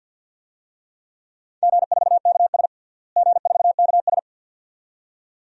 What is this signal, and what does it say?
Morse code, heard as a keyed tone, about 46 wpm, 700 Hz. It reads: G4KS G4KS
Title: DXCC-100 Call Signs - 4 Characters - Repeated 2x - 50wpm